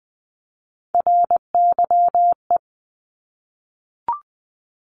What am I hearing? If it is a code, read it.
Morse code, heard as a keyed tone, about 20 wpm, 700 Hz. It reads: RYE
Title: Multi-letter Phonemes Repeated 3x 20wpm